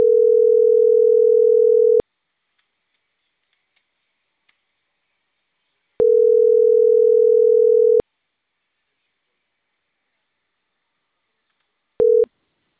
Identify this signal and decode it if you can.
Ringback tone (North America), 440 + 480 Hz.